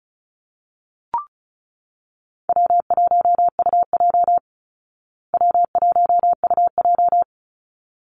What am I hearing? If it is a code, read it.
Morse code, heard as a keyed tone, about 34 wpm, 700 Hz. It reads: W1UJ W1UJ
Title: Call Signs Repeated 3x - 35wpm